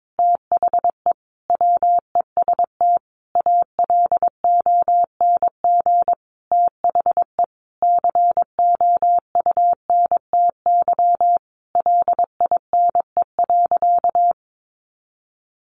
Morse code, heard as a keyed tone, about 22 wpm, 700 Hz. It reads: THE WEST ALONG THE COUNTY LINE.
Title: Sentences from Top 1200 Words 22wpm